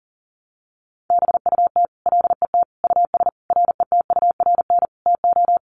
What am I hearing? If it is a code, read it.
Morse code, heard as a keyed tone, about 40 wpm, 700 Hz. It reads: BUT LET US RETURN TO
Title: Sentences from Top 800 Words 40wpm